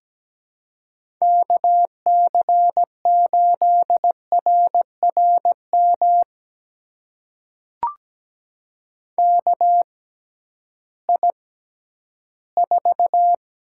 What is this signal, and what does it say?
Morse code, heard as a keyed tone, about 17 wpm, spaced as for 11 wpm, 700 Hz. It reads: KC8RRM KI4
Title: USA Call Signs - 6 Characters - 6x Character Spacing - 17wpm